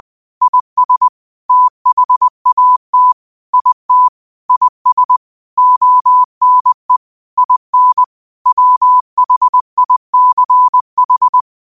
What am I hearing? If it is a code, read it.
Morse code, heard as a keyed tone, about 20 wpm, 1000 Hz. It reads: IS THAT IT IS ONE IN WHICH
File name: qotdfiles / 20wpm